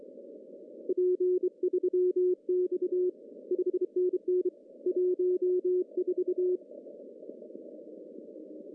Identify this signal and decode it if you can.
Morse code, heard as a keyed tone, about 24 wpm, 360 Hz. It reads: P3X 5NN 14
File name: CW / cqww-cw-k5tr